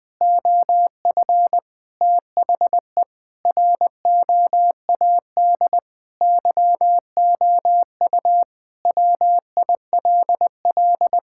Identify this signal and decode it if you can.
Morse code, heard as a keyed tone, about 20 wpm, 700 Hz. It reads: OF THE ROAD YOU WILL
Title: Sentences with 3-Word Phrases - 20wpm